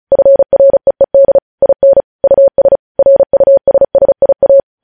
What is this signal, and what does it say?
Morse code, heard as a keyed tone, about 36 wpm, 550 Hz. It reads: FREED IN US RUSSIA
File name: podcast / News headlines-1x-35x35